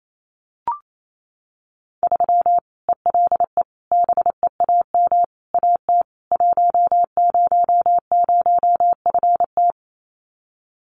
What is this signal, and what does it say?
Morse code, heard as a keyed tone, about 28 wpm, 700 Hz. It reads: 3 ELE BEAM AT 100FT
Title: Common QSO Phrases 28wpm